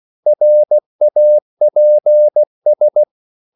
Morse code, heard as a keyed tone, about 16 wpm, 600 Hz. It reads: RAPS
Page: Korte danske ord 16wpm